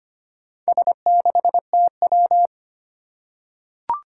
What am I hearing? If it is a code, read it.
Morse code, heard as a keyed tone, about 25 wpm, 700 Hz. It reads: S6TW
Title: Four Challenging Letter-Number Combinations 25wpm